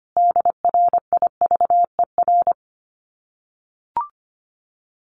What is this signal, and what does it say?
Morse code, heard as a keyed tone, about 25 wpm, 700 Hz. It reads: DRIVER
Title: Common QSO Phrases Encore 25wpm